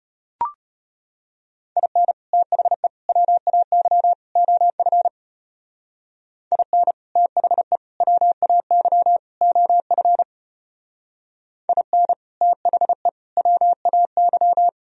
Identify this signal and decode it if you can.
Morse code, heard as a keyed tone, about 30 wpm, 700 Hz. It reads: IN THE WAY OF IN THE WAY OF IN THE WAY
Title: Speed-Racing - Sets of 4 Words 25wpm